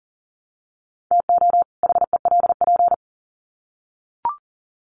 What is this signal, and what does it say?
Morse code, heard as a keyed tone, about 40 wpm, 700 Hz. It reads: TO HELP
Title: Sets of 2 Words 40wpm